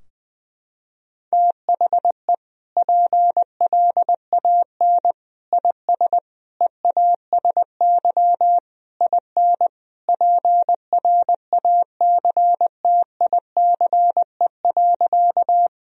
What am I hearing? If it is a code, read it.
Morse code, heard as a keyed tone, about 21 wpm, 700 Hz. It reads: THE PLAN IS EASY IN PRACTICE.
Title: Sentences from Top 1000 Words - Review Part 4 - 20wpm